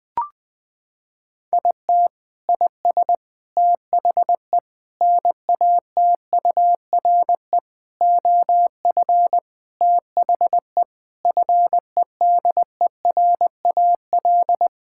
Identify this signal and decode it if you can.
Morse code, heard as a keyed tone, about 20 wpm, 700 Hz. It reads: IT IS THE NATURE OF THE FEDERAL
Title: Sentences from Top 1000 Words - Review Part 5 - 20wpm